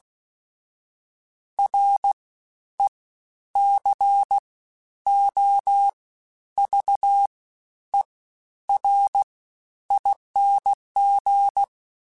Morse code, heard as a keyed tone, about 16 wpm, spaced as for 10 wpm, 800 Hz. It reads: RECOVERING
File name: Special INGs-mp3s / Special INGs-mp3s / SlowToFaster ing @15